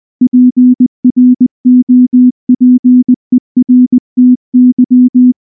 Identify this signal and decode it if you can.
Morse code, heard as a keyed tone, about 19 wpm, 250 Hz. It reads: PROPERTY
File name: morse / morse-training-20WPM-5